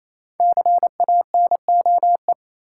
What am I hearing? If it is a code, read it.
Morse code, heard as a keyed tone, about 27 wpm, 690 Hz. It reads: CANOE
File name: Easy Words - Speed-Racing with Hints - 22wpm